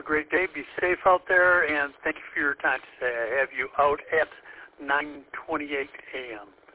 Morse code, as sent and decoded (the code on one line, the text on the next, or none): none